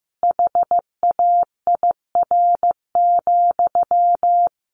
Morse code, heard as a keyed tone, about 15 wpm, 700 Hz. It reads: HAIR,
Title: A Sound Of Thunder 15wpm